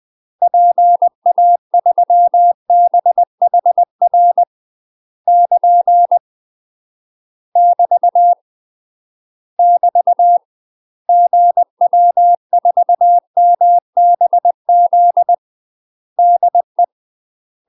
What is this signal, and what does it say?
Morse code, heard as a keyed tone, about 20 wpm, 700 Hz. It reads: PA3BHR <KN> <BT> <BT> GW4MBZ DE
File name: UK MASTER QSOs TRAINING / 20WPM / UK MASTER QSOs 0026